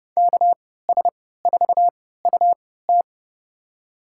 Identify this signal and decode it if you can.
Morse code, heard as a keyed tone, about 29 wpm, spaced as for 17 wpm, 700 Hz. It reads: KS4UT
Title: USA Call Signs - 5 Characters - 3x Character Spacing - 30wpm